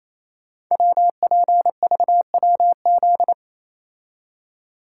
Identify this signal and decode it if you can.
Morse code, heard as a keyed tone, about 27 wpm, 700 Hz. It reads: WPVWZ
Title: Code Groups - 5 Letters - 28wpm